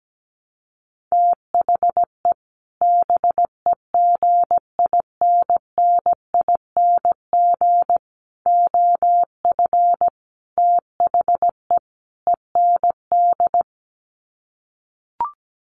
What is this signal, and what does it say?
Morse code, heard as a keyed tone, about 17 wpm, 700 Hz. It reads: THE BEGINNING OF THE END
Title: Sets of 5 Words Encore2 17wpm